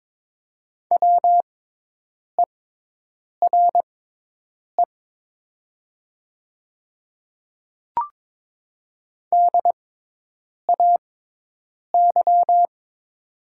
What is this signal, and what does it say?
Morse code, heard as a keyed tone, about 22 wpm, spaced as for 8 wpm, 700 Hz. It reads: WERE DAY
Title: 6x Character Spacing - Top 100 Words - 22wpm